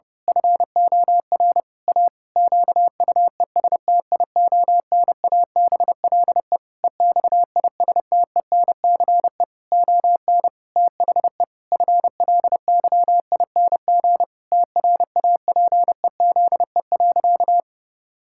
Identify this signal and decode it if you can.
Morse code, heard as a keyed tone, about 32 wpm, 700 Hz. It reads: FOR A QUESTIONABLE EXISTENCE ON THE FLYING TRAPEZE.
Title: Pangram Challenge v2 - 30wpm